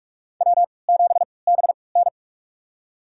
Morse code, heard as a keyed tone, about 41 wpm, spaced as for 25 wpm, 700 Hz. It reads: W7BN
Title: USA Call Signs - 3x Character Spacing - 45wpm